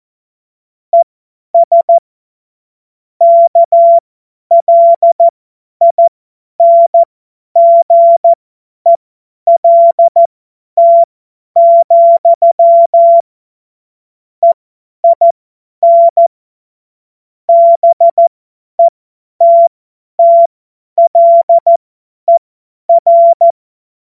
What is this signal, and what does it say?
Morse code, heard as a keyed tone, about 14 wpm, spaced as for 10 wpm, 680 Hz. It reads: ES KLINGELT, EIN BETTLER
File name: Morsekurs / Witze / Witz 09. Februar 2021, 50 F70